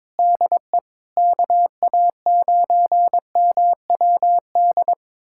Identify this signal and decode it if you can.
Morse code, heard as a keyed tone, about 23 wpm, 700 Hz. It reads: DE KA9MWD
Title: USA Callsign DE Callsign - 6 Characters - 22wpm